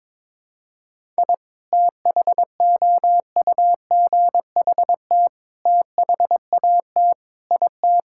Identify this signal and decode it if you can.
Morse code, heard as a keyed tone, about 22 wpm, 700 Hz. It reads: I THOUGHT THAT IT
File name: practice / 22wpm / Sentences with 4-Word Phrases - 22wpm